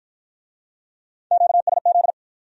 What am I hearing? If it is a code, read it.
Morse code, heard as a keyed tone, about 50 wpm, 700 Hz. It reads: QSZ